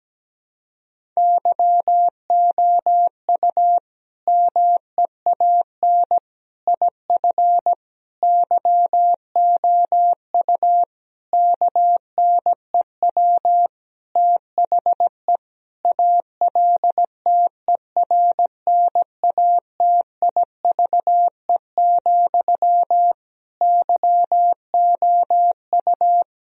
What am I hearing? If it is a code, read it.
Morse code, heard as a keyed tone, about 17 wpm, 700 Hz. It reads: YOU MEAN IF YOU KNEW THE ALTERNATIVE, YOU
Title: Sentences from Top 1000 Words - Review Part 4 - 17wpm